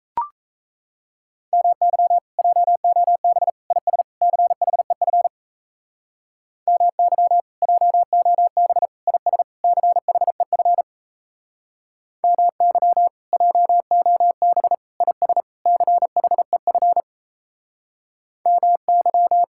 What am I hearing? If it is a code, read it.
Morse code, heard as a keyed tone, about 37 wpm, 700 Hz. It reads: MY JOB IS CHEF MY JOB IS CHEF MY JOB IS CHEF MY